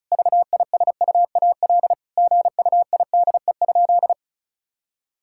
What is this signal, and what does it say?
Morse code, heard as a keyed tone, about 36 wpm, 700 Hz. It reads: VISUAL GUIDE?
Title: Sentences from Top 1000 Words - Review Part 6 - 35wpm